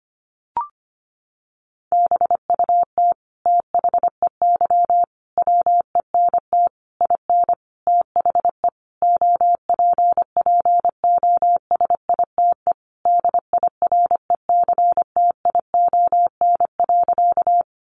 Morse code, heard as a keyed tone, about 25 wpm, 700 Hz. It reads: BUT THEY WENT IN THE OPPOSITE DIRECTION.
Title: Sentences from Top 1300 Words 25wpm